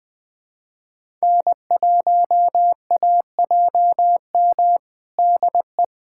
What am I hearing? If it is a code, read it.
Morse code, heard as a keyed tone, about 19 wpm, 700 Hz. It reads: N1AJM DE
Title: USA Callsign DE Callsign 20wpm